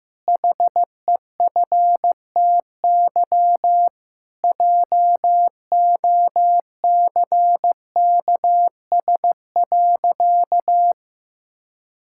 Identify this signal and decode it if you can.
Morse code, heard as a keyed tone, about 15 wpm, 700 Hz. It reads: HEFTY JOCKS.
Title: Pangram Challenge v2 - 15wpm